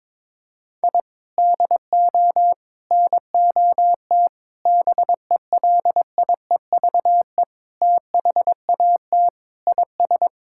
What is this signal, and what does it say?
Morse code, heard as a keyed tone, about 21 wpm, 700 Hz. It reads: I DO NOT BELIEVE THAT IS
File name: practice / 22wpm / Sentences with 4-Word Phrases - 22wpm